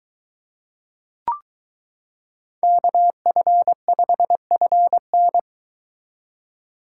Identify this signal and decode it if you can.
Morse code, heard as a keyed tone, about 23 wpm, 700 Hz. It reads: KF5FN